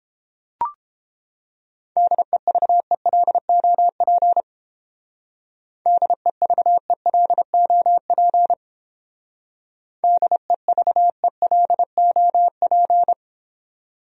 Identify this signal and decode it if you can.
Morse code, heard as a keyed tone, about 30 wpm, 700 Hz. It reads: DEVELOP DEVELOP DEVELOP